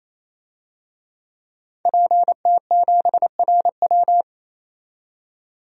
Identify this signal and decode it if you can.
Morse code, heard as a keyed tone, about 28 wpm, 700 Hz. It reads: PT7RW
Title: DXCC-100 Call Signs - 5 Characters - 28wpm